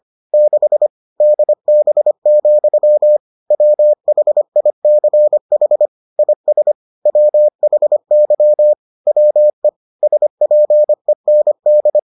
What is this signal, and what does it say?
Morse code, heard as a keyed tone, about 25 wpm, 600 Hz. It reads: 6 DB, WHICH IS WHY WE SPEND